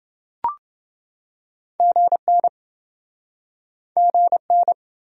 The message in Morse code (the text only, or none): GN GN